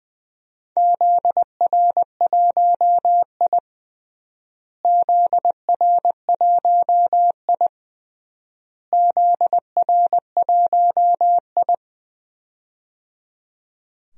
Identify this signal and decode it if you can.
Morse code, heard as a keyed tone, about 20 wpm, 700 Hz. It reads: ZR1I ZR1I ZR1I
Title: DXCC-100 Call Signs - 4 Characters - Repeated 3x - 20wpm